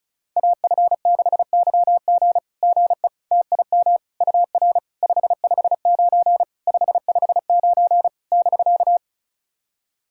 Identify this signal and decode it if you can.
Morse code, heard as a keyed tone, about 36 wpm, 690 Hz. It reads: AF6YG GE TIM UR 559 559 <BK>
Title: SOTA QSO Practice 35wpm